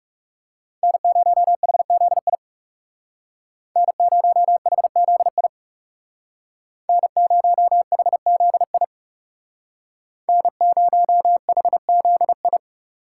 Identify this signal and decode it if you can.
Morse code, heard as a keyed tone, about 38 wpm, 700 Hz. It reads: N0HZI N0HZI N0HZI N0HZI